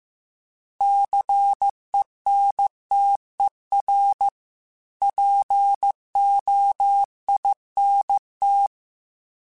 Morse code, heard as a keyed tone, about 15 wpm, 800 Hz. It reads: CENTER POINT
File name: fromSMs / FromSMs mp3s / Cities@13